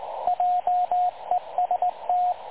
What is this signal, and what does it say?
Morse code, heard as a keyed tone, about 20 wpm, 720 Hz, with radio noise. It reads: JEST
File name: dlazaawansowanych / nagraniazpasma / websdr recording 3565.0kHz